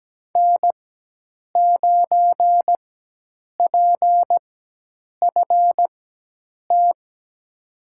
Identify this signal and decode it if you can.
Morse code, heard as a keyed tone, about 17 wpm, spaced as for 8 wpm, 700 Hz. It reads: N9PFT
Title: USA Call Signs - 5 Characters - 4x Character Spacing - 17wpm